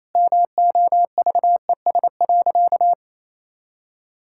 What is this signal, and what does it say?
Morse code, heard as a keyed tone, about 28 wpm, 700 Hz. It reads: MOVES.
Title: Pangram Challenge v2 - 28wpm